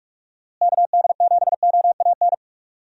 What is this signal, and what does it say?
Morse code, heard as a keyed tone, about 41 wpm, 700 Hz. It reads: KD7OAN
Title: USA Call Signs - 6 Characters - Repeated 3x - 45wpm